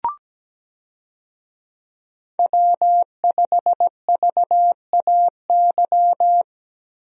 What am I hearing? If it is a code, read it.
Morse code, heard as a keyed tone, about 17 wpm, 700 Hz. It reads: W5VAY
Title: USA Call Signs - 5 Characters - Repeated 2x - 17wpm